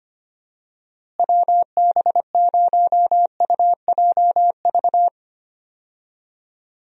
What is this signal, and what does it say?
Morse code, heard as a keyed tone, about 25 wpm, 700 Hz. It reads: WB0UJV
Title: USA Call Signs - 6 Characters - Repeated 2x - 25wpm